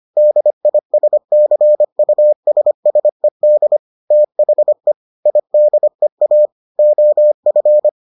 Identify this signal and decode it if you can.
Morse code, heard as a keyed tone, about 25 wpm, 600 Hz. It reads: DISCUSSED THE IDEA OF